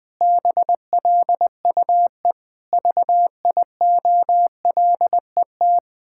Morse code, heard as a keyed tone, about 20 wpm, 700 Hz. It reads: BLUE VIOLET
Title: Pangram Challenge v2 - 20wpm